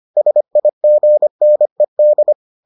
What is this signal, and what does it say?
Morse code, heard as a keyed tone, about 25 wpm, 600 Hz. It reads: SIGNED